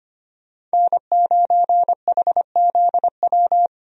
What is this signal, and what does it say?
Morse code, heard as a keyed tone, about 25 wpm, 700 Hz. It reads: N9HZW
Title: USA Call Signs - 5 Characters - 25wpm